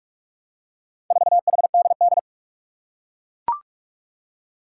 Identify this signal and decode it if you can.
Morse code, heard as a keyed tone, about 41 wpm, 700 Hz. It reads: 4HDD
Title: Four Challenging Letter-Number Combinations 45wpm